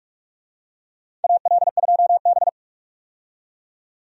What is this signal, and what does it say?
Morse code, heard as a keyed tone, about 45 wpm, 700 Hz. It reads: AL2B
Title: USA Call Signs - 4 Characters - 45wpm